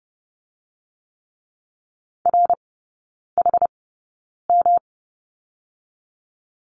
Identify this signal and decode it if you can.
Morse code, heard as a keyed tone, about 30 wpm, 700 Hz. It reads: R H M
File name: Lesson 19e - Letter w - Three Characters 30wpm